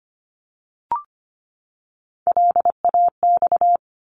Morse code, heard as a keyed tone, about 25 wpm, 700 Hz. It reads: LAX
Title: ARRL Sections 25wpm